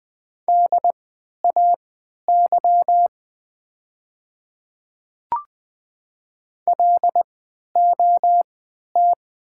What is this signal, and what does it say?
Morse code, heard as a keyed tone, about 20 wpm, spaced as for 11 wpm, 700 Hz. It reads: DAY LOT
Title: 3x Character Spacing - Top 500 Words - 20wpm